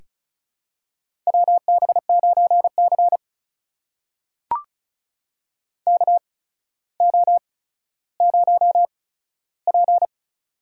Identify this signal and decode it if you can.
Morse code, heard as a keyed tone, about 36 wpm, spaced as for 15 wpm, 700 Hz. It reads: WB9C KO0P
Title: USA Call Signs - 4 Characters - 8x Character Spacing - 35wpm